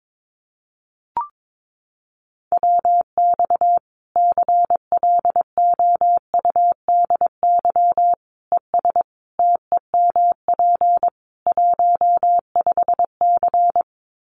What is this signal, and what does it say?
Morse code, heard as a keyed tone, about 22 wpm, 700 Hz. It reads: WX CLOUDY ES TEMP 15C
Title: Common QSO Phrases 22wpm